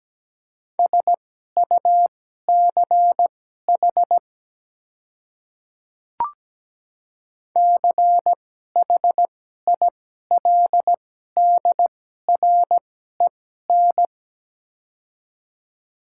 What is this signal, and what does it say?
Morse code, heard as a keyed tone, about 17 wpm, spaced as for 12 wpm, 700 Hz. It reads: SUCH CHILDREN
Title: 2x Character Spacing - Top 300 Words - 17wpm